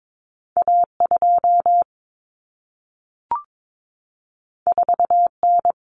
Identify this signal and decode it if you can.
Morse code, heard as a keyed tone, about 22 wpm, 700 Hz. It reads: A2 4N
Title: Common Cut Numbers - 2-Digit 22wpm